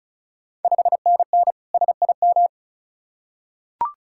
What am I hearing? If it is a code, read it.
Morse code, heard as a keyed tone, about 36 wpm, 700 Hz. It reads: 5NN SIM
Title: Ontario QSO Party Exchange 35wpm